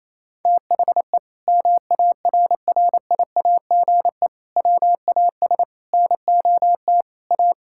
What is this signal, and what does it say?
Morse code, heard as a keyed tone, about 28 wpm, 700 Hz. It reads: THE MARRIAGE WAS NOT A
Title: Sentences from Top 1000 Words - Review Part 5 - 28wpm